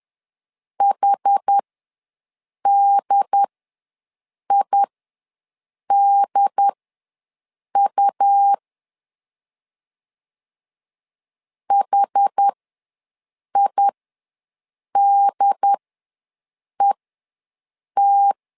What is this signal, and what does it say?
Morse code, heard as a keyed tone, about 10 wpm, spaced as for 6 wpm, 800 Hz. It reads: HDIDU HIDET
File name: LEZIONI / 2 UHDN / 2 UHD ws 10wpm 2058